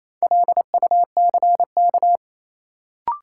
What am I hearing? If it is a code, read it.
Morse code, heard as a keyed tone, about 27 wpm, 700 Hz. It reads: LUCK
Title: Sentences from Top 1500 Words - New Word Drill Alt - 28wpm